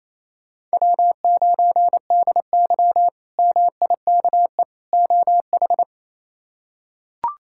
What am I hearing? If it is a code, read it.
Morse code, heard as a keyed tone, about 28 wpm, 700 Hz. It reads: W9DY MIKE OH
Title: K1USN SST QSO Practice - 28wpm